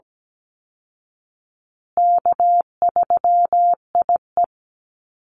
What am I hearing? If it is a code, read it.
Morse code, heard as a keyed tone, about 17 wpm, 700 Hz. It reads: K3IE
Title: Call Signs Repeated 3x - 17wpm